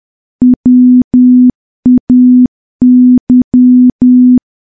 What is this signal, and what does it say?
Morse code, heard as a keyed tone, about 10 wpm, 250 Hz. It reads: WAY
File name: morse / morse-training-10WPM